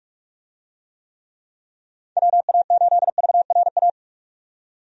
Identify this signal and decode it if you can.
Morse code, heard as a keyed tone, about 46 wpm, 700 Hz. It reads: WA8VRA
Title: USA Call Signs - 6 Characters - 45wpm